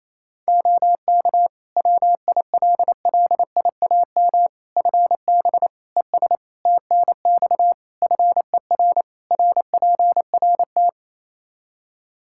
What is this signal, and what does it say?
Morse code, heard as a keyed tone, about 28 wpm, 700 Hz. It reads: OK WILLIAM FB ES TNX FER RPRT
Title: Practice QSO - Exchange 2 - Info - 28wpm